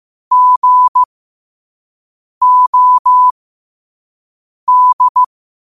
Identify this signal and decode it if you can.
Morse code, heard as a keyed tone, about 15 wpm, spaced as for 5 wpm, 1000 Hz. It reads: GOD
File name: qotdfiles / 5wpm